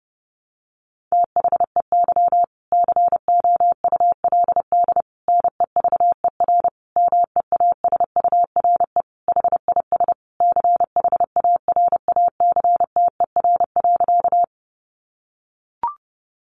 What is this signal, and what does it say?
Morse code, heard as a keyed tone, about 30 wpm, 700 Hz. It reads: THEY COULD NEVER MEASURE HIS CHARACTER.
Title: Sentences from Top 1200 Words 30wpm